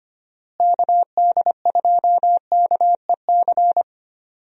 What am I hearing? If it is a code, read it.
Morse code, heard as a keyed tone, about 25 wpm, 700 Hz. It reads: KD2KEC
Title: USA Call Signs - 6 Characters - Repeated 2x - 25wpm